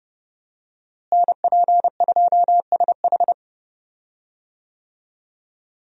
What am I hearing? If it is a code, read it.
Morse code, heard as a keyed tone, about 32 wpm, 700 Hz. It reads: NP2SH
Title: USA Call Signs - 5 Characters - Repeated 2x - 30wpm